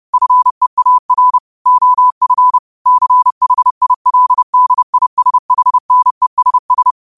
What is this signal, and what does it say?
Morse code, heard as a keyed tone, about 30 wpm, 1000 Hz. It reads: FEAR OF CHILDISHNESS
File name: qotdfiles / 30wpm